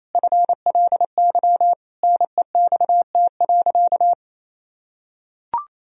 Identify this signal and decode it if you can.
Morse code, heard as a keyed tone, about 27 wpm, 700 Hz. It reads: FLY NEXT.
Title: Sentences from Top 1000 Words - Review Part 2 - 28wpm